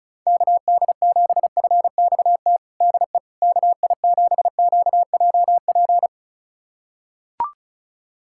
Morse code, heard as a keyed tone, about 36 wpm, 700 Hz. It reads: KD7FXT DE KI7QJP
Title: USA Callsign DE Callsign - 6 Characters - 35wpm